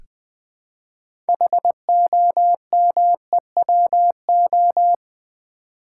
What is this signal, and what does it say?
Morse code, heard as a keyed tone, about 20 wpm, 700 Hz. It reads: HOMEWO
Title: Build A Word - 3 Letters At A Time - 8-Letter Words - 20wpm